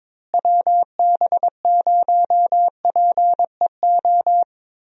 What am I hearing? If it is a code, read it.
Morse code, heard as a keyed tone, about 22 wpm, 700 Hz. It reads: WB0PEO
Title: USA Call Signs - 6 Characters - Repeated 2x - 22wpm